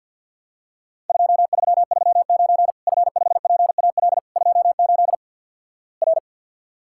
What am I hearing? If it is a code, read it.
Morse code, heard as a keyed tone, about 50 wpm, 700 Hz. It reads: 2339 F5PAL 28 R
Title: Worked All Europe DX Contest - QTC Exchange - 50wpm